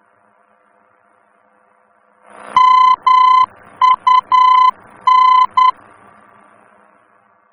Morse code, heard as a keyed tone, about 10 wpm, 1020 Hz. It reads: MUN
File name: germany / mx / MUN-D-VOR